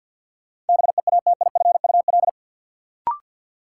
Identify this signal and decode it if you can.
Morse code, heard as a keyed tone, about 46 wpm, 700 Hz. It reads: BEATIFUL
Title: Sentences from Top 1300 Words - New Word Drill Alt - 50wpm